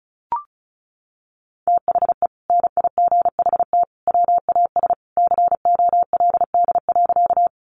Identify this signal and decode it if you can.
Morse code, heard as a keyed tone, about 34 wpm, 700 Hz. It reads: THE NIGHT WAS COLD.
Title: Sentences from Top 400 Words 35wpm